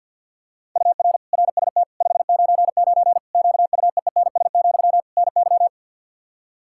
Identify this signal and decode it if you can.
Morse code, heard as a keyed tone, about 51 wpm, 700 Hz. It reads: UR RST 599 QUEENS, NY
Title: Practice QSO - Exchange 1 - Beginning - 50wpm